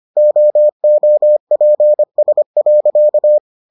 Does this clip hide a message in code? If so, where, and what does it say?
Yes: Morse code from 0.2 to 3.4 s, heard as a keyed tone, about 25 wpm, 600 Hz. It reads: OOPS.